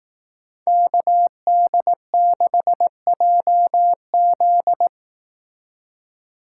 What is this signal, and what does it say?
Morse code, heard as a keyed tone, about 18 wpm, 700 Hz. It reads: KD6JZ
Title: Speed Racing - USA 2x2 Call Signs 15wpm